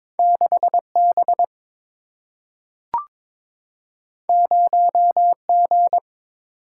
Morse code, heard as a keyed tone, about 22 wpm, 700 Hz. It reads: 6 B 0 G
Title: Single-Letter Single-Number 22wpm